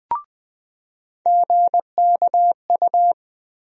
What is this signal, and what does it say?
Morse code, heard as a keyed tone, about 20 wpm, 700 Hz. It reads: GKU
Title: USA Call Sign Postfixes - 3 Letters 20wpm